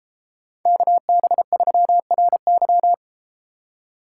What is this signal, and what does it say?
Morse code, heard as a keyed tone, about 31 wpm, 700 Hz. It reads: KB3RY